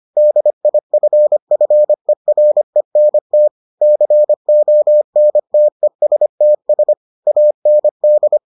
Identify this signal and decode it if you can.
Morse code, heard as a keyed tone, about 25 wpm, 600 Hz. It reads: DIFFERENT CONTESTS AND